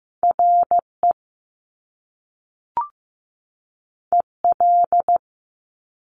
Morse code, heard as a keyed tone, about 15 wpm, 700 Hz. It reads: RE EL
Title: Top Two Letter Combinations in Words 15wpm